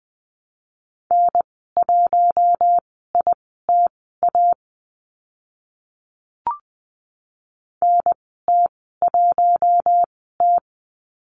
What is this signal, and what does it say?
Morse code, heard as a keyed tone, about 20 wpm, spaced as for 14 wpm, 700 Hz. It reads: N1ITA NT1T
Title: 042 Lesson 8g - Number 1 - Call Signs 20wpm